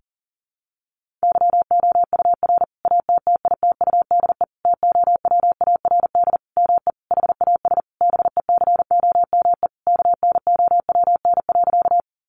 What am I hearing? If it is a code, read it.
Morse code, heard as a keyed tone, about 40 wpm, 700 Hz. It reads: YOUR ATTITUDE TOWARD ME HAS BECOME KNOWN.
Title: Sentences from Top 1300 Words 40wpm